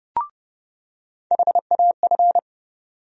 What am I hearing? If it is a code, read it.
Morse code, heard as a keyed tone, about 30 wpm, 700 Hz. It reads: HAF
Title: USA Call Sign Postfixes - 3 Letters 30wpm